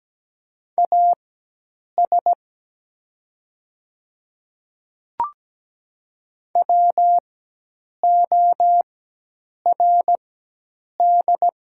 Morse code, heard as a keyed tone, about 17 wpm, spaced as for 8 wpm, 700 Hz. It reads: AS WORD
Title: 4x Character Spacing - Top 100 Words - 17wpm